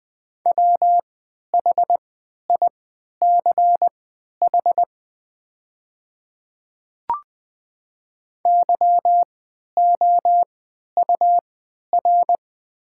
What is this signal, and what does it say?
Morse code, heard as a keyed tone, about 20 wpm, spaced as for 11 wpm, 690 Hz. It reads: WHICH YOUR
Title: 3x Character Spacing - Top 200 Words - 20wpm